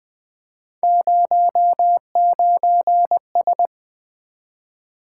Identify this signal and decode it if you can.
Morse code, heard as a keyed tone, about 20 wpm, 700 Hz. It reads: 09S
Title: Three Letter-Number Combinations - 20wpm